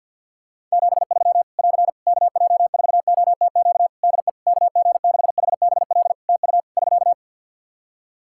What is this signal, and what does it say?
Morse code, heard as a keyed tone, about 50 wpm, 700 Hz. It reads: 73 <AR> KJ4OTQ DE KG6HBL TU <SK>
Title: Practice QSO - Exchange 4 - Ending - 50wpm